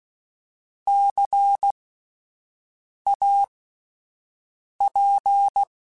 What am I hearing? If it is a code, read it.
Morse code, heard as a keyed tone, about 16 wpm, spaced as for 5 wpm, 800 Hz. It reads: CAP